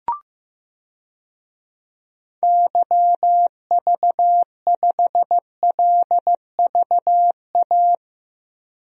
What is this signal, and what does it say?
Morse code, heard as a keyed tone, about 15 wpm, 700 Hz. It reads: YV5LVA
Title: DXCC-100 Call Signs - 6 Characters - 15wpm